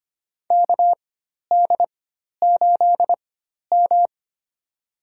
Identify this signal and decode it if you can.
Morse code, heard as a keyed tone, about 25 wpm, spaced as for 12 wpm, 700 Hz. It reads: KD8M
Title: USA Call Signs - 4 Characters - 4x Character Spacing - 25wpm